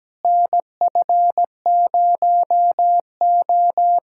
Morse code, heard as a keyed tone, about 17 wpm, 700 Hz. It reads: NF0O
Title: USA Callsign DE Callsign - 4 Characters - 17wpm